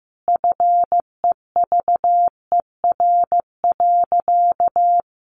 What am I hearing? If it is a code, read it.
Morse code, heard as a keyed tone, about 15 wpm, 700 Hz. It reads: FEVER.
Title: A Sound Of Thunder 15wpm